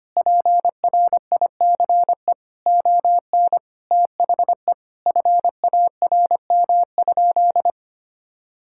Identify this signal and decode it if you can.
Morse code, heard as a keyed tone, about 25 wpm, 700 Hz. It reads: PRICE ON THE FARM?
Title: Sentences from Top 1000 Words - Review Part 6 - 25wpm